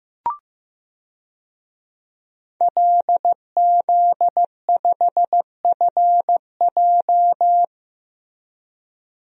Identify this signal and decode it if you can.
Morse code, heard as a keyed tone, about 15 wpm, 700 Hz. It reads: LZ5FJ
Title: DXCC-100 Call Signs - 5 Characters - Repeated 2x - 15wpm